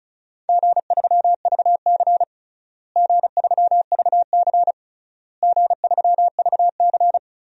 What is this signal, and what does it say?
Morse code, heard as a keyed tone, about 36 wpm, 700 Hz. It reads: G3VC G3VC G3VC
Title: DXCC-100 Call Signs - 4 Characters - Repeated 3x - 35wpm